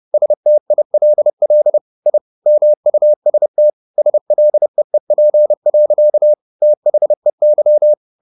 Morse code, heard as a keyed tone, about 30 wpm, 600 Hz. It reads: STILL I MUST SLEEP. THEY